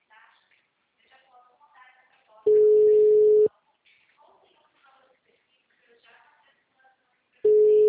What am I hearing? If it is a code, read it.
Ringback tone (Europe, 425 Hz), with speech.
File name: out- -2016- - - .38023